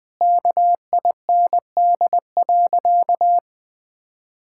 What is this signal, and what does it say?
Morse code, heard as a keyed tone, about 20 wpm, 700 Hz. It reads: KIND.
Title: Sentences with 3-Word Phrases - 20wpm